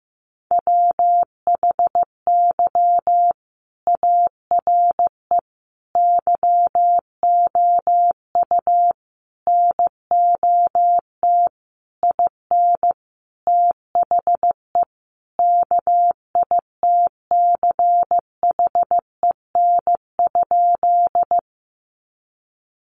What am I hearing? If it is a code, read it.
Morse code, heard as a keyed tone, about 15 wpm, 700 Hz. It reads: WHY ARE YOU NOT IN THE KITCHEN?
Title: Sentences from Top 1400 Words 15wpm